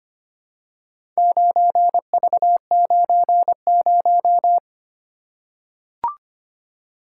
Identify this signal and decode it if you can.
Morse code, heard as a keyed tone, about 25 wpm, 700 Hz. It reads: 9V90
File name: practice / 25wpm / Four Letter-Number Combinations - 25wpm